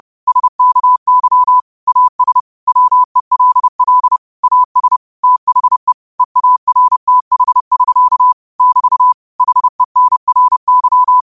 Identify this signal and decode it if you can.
Morse code, heard as a keyed tone, about 30 wpm, 1000 Hz. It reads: SKY AS WELL AS THE EARTH<SK> <BT> HENRY